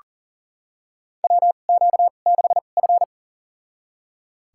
Morse code, heard as a keyed tone, about 40 wpm, spaced as for 29 wpm, 700 Hz. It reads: WQ6F
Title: USA Call Signs - 2x Character Spacing - 40wpm